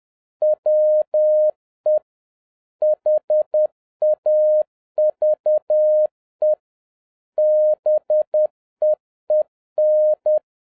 Morse code, heard as a keyed tone, about 10 wpm, 610 Hz. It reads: WE HAVE BEEN